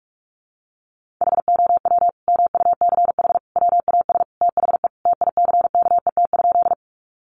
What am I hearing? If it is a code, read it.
Morse code, heard as a keyed tone, about 44 wpm, 700 Hz. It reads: HOW MUCH WAS THE TICKET?
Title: Sentences from Top 1400 Words 45wpm